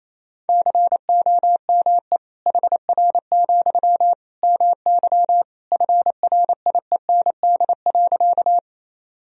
Morse code, heard as a keyed tone, about 28 wpm, 700 Hz. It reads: COME HR, MY FRIEND.
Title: Sentences from Top 300 Words and Abbreviations 28wpm